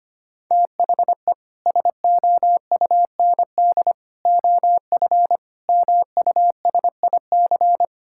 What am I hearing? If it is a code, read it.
Morse code, heard as a keyed tone, about 25 wpm, 700 Hz. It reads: THE SOUND OF MUSIC
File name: practice / 25wpm / Sentences with 3-Word Phrases - 25wpm